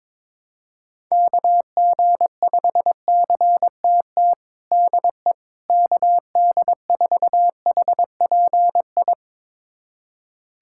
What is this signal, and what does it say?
Morse code, heard as a keyed tone, about 22 wpm, 700 Hz. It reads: KG5CTT DE KD4HPI
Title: USA Callsign DE Callsign 22wpm